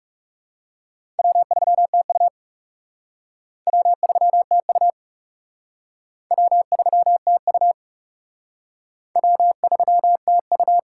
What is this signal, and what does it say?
Morse code, heard as a keyed tone, about 36 wpm, 700 Hz. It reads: W3TU W3TU W3TU W3TU
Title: Speed Racing - USA 1x2 Call Signs 30wpm